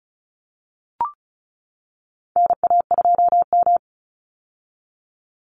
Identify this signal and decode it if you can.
Morse code, heard as a keyed tone, about 34 wpm, 700 Hz. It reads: NA2M
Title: Call Signs - 35wpm